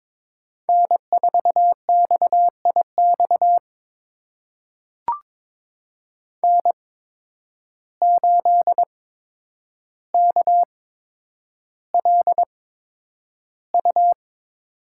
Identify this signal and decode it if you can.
Morse code, heard as a keyed tone, about 21 wpm, spaced as for 9 wpm, 700 Hz. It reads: N4XIX N8KLU
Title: USA Call Signs - 5 Characters - 8x Character Spacing - 22wpm